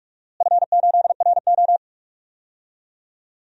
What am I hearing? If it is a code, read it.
Morse code, heard as a keyed tone, about 47 wpm, 700 Hz. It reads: F8RO
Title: DXCC-100 Call Signs - 4 Characters - Repeated 3x - 45wpm